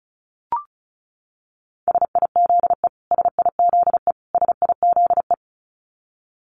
Morse code, heard as a keyed tone, about 34 wpm, 700 Hz. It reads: SIZE SIZE SIZE
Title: Top 500 Words Repeated 35wpm